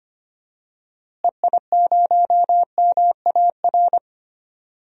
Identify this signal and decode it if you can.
Morse code, heard as a keyed tone, about 25 wpm, 700 Hz. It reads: EI0MAR
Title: DXCC-100 Call Signs - 6 Characters - Repeated 3x - 25wpm